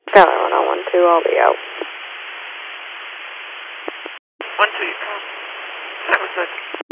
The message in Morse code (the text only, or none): none